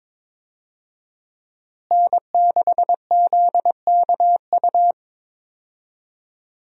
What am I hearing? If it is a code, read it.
Morse code, heard as a keyed tone, about 22 wpm, 700 Hz. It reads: N6ZKU
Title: USA Call Signs - 5 Characters - Repeated 2x - 22wpm